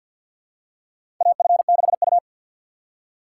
Morse code, heard as a keyed tone, about 53 wpm, 700 Hz. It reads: AF6U
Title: USA Call Signs - 6x Character Spacing - 50wpm